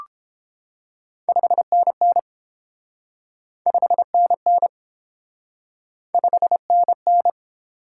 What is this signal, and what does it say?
Morse code, heard as a keyed tone, about 30 wpm, 700 Hz. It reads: 5NN 5NN 5NN